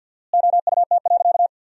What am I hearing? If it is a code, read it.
Morse code, heard as a keyed tone, about 46 wpm, 700 Hz. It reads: OUT.
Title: 4x Word-Spacing - Sentences from Top 200 Words - 50wpm